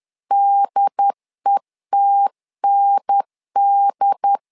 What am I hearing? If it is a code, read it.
Morse code, heard as a keyed tone, about 10 wpm, 800 Hz. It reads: DETND
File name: LEZIONI / 4 Q1 10wpm 2448